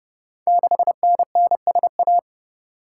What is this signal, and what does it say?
Morse code, heard as a keyed tone, about 30 wpm, 700 Hz. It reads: 6NNSA
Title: Code Groups - 5 Letter:Numbers - 30wpm